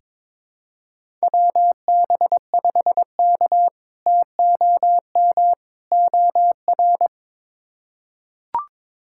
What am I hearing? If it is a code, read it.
Morse code, heard as a keyed tone, about 22 wpm, 700 Hz. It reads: WB5K TOM OR
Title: K1USN SST QSO Practice - 22wpm